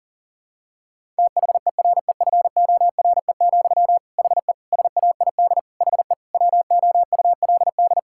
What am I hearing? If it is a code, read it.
Morse code, heard as a keyed tone, about 40 wpm, 700 Hz. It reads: THEREFORE, HE SAID HE WOULD
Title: Sentences from Top 1000 Words - Review Part 5 - 40wpm